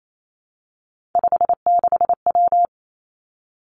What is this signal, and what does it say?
Morse code, heard as a keyed tone, about 28 wpm, 700 Hz. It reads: 56W
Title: Three Challenging Letter-Number Combinations 28wpm